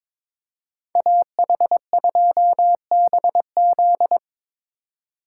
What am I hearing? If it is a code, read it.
Morse code, heard as a keyed tone, about 26 wpm, 700 Hz. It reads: AH2BZ
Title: Speed Racing - USA 2x2 Call Signs 22wpm